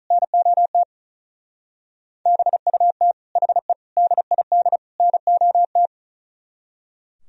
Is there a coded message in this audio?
Yes: Morse code from 0.1 to 5.9 s, heard as a keyed tone, about 46 wpm, 700 Hz. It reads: NOT BUT HE DID NOT